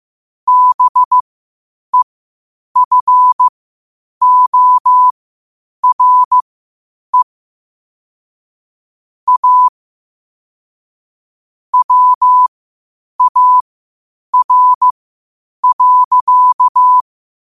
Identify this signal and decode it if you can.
Morse code, heard as a keyed tone, about 15 wpm, spaced as for 8 wpm, 1000 Hz. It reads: BEFORE A WAR.
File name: qotdfiles / 8wpm